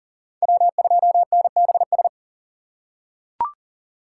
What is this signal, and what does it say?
Morse code, heard as a keyed tone, about 39 wpm, 700 Hz. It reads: W2NBS
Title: USA Call Signs - 5 Characters - 3x Character Spacing - 40wpm